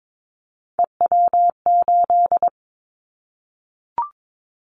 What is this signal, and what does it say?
Morse code, heard as a keyed tone, about 22 wpm, 700 Hz. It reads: EW8
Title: Callsign Prefixes for Top 50 DXCC - 22wpm